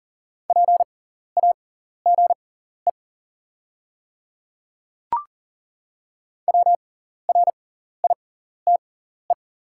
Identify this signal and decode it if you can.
Morse code, heard as a keyed tone, about 40 wpm, spaced as for 14 wpm, 700 Hz. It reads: PAGE WRITE
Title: 6x Character Spacing - Top 300 Words - 40wpm